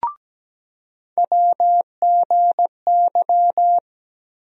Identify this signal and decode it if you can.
Morse code, heard as a keyed tone, about 17 wpm, 700 Hz. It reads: WGY
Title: USA Call Sign Postfixes - 3 Letters 17wpm